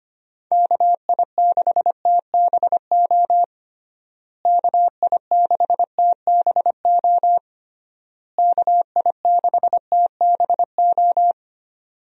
Morse code, heard as a keyed tone, about 25 wpm, 700 Hz. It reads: KI6TBO KI6TBO KI6TBO
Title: USA Call Signs - 6 Characters - Repeated 3x - 25wpm